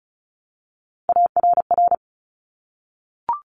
Morse code, heard as a keyed tone, about 36 wpm, 700 Hz. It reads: ARR
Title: Top Three Letter Combinations in Words 35wpm